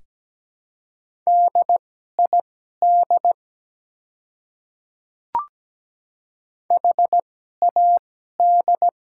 Morse code, heard as a keyed tone, about 17 wpm, spaced as for 12 wpm, 700 Hz. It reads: DID HAD
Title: 2x Character Spacing - Top 100 Words - 17wpm